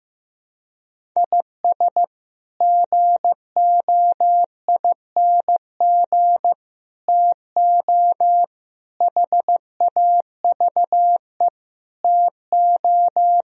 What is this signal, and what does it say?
Morse code, heard as a keyed tone, about 15 wpm, 700 Hz. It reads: IS GOING TO HAVE TO
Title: Speed-Racing - Sets of 5 Words 15wpm